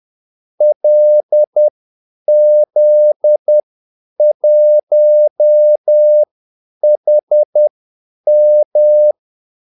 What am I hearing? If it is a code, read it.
Morse code, heard as a keyed tone, about 10 wpm, spaced as for 8 wpm, 600 Hz. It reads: LZ1HM
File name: Callsigns-10-8wpm